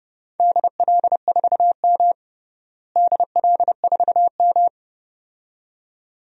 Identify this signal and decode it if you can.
Morse code, heard as a keyed tone, about 30 wpm, 700 Hz. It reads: DL4M DL4M
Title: DXCC-100 Call Signs - 4 Characters - Repeated 2x - 30wpm